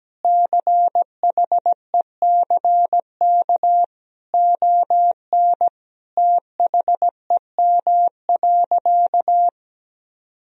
Morse code, heard as a keyed tone, about 17 wpm, 700 Hz. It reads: CHECK ON THEM.
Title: Sentences from Top 1000 Words - Review Part 3 - 17wpm